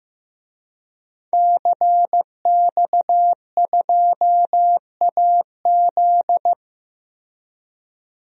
Morse code, heard as a keyed tone, about 15 wpm, 700 Hz. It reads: CX2AZ
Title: DXCC-100 Call Signs - 5 Characters - 15wpm